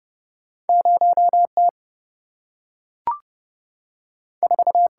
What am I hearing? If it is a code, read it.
Morse code, heard as a keyed tone, about 30 wpm, 700 Hz. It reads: 0T 4
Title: Single-Letter Single-Number 30wpm